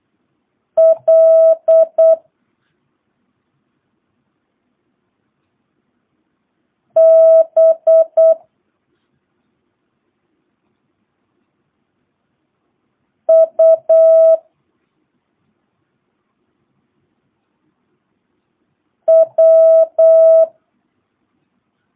Morse code, heard as a keyed tone, about 8 wpm, 650 Hz. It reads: L B U W
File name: jh8cbh / 2022cw